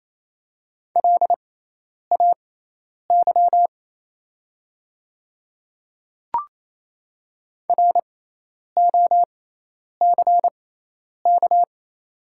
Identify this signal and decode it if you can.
Morse code, heard as a keyed tone, about 28 wpm, spaced as for 10 wpm, 700 Hz. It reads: LAY ROCK
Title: 6x Character Spacing - Top 500 Words - 28wpm